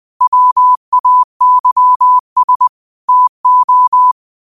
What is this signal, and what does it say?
Morse code, heard as a keyed tone, about 20 wpm, 1000 Hz. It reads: WAYS TO
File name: qotdfiles / 20wpm